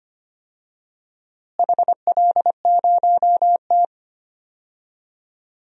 Morse code, heard as a keyed tone, about 25 wpm, 700 Hz. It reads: HL0T
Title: DXCC-100 Call Signs - 4 Characters - Repeated 3x - 25wpm